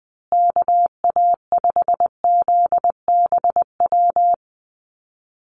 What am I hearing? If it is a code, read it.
Morse code, heard as a keyed tone, about 20 wpm, 700 Hz. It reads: KA5ZBW
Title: USA Call Signs Repeated 2x 20wpm